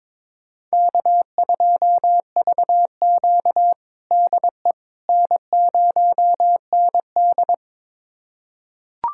Morse code, heard as a keyed tone, about 22 wpm, 700 Hz. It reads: K2VQ DE N0ND
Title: USA Callsign DE Callsign - 4 Characters - 22wpm